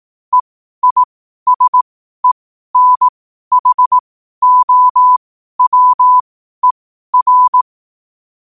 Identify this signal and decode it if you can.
Morse code, heard as a keyed tone, about 18 wpm, spaced as for 12 wpm, 1000 Hz. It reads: EISENHOWER
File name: qotdfiles / 12wpm